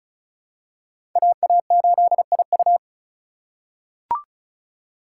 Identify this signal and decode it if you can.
Morse code, heard as a keyed tone, about 36 wpm, 700 Hz. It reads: AA8IU
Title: USA Call Signs - 5 Characters - 2x Character Spacing - 35wpm